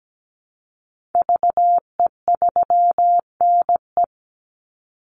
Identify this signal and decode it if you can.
Morse code, heard as a keyed tone, about 17 wpm, 700 Hz. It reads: VE3NE
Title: Call Signs Repeated 2x - 17wpm